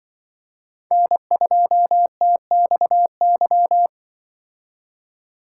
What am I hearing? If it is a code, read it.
Morse code, heard as a keyed tone, about 23 wpm, 700 Hz. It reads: N2TXY
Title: Speed Racing - USA 1x3 Call Signs 20wpm